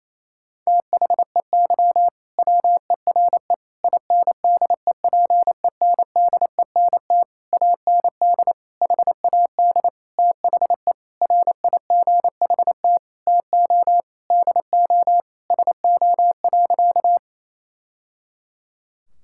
Morse code, heard as a keyed tone, about 28 wpm, 700 Hz. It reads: THEY WERE INDEPENDENT AND HAD THE RIGHT TO DO SO.
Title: Sentences from Top 1000 Words - Review Part 6 - 28wpm